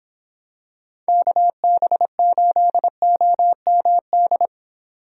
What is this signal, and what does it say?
Morse code, heard as a keyed tone, about 24 wpm, 700 Hz. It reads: KB8OMD